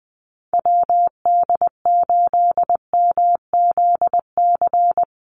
Morse code, heard as a keyed tone, about 20 wpm, 700 Hz. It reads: WD8MZC
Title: USA Call Signs 20wpm